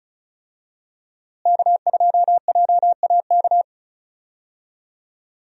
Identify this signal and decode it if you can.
Morse code, heard as a keyed tone, about 36 wpm, 700 Hz. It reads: K2JAK
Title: USA Call Signs - 5 Characters - Repeated 2x - 35wpm